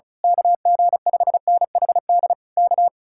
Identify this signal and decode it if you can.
Morse code, heard as a keyed tone, about 33 wpm, 700 Hz. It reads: KG5NHD K
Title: Speed-Racing Common QSO Phrases Encore - 35wpm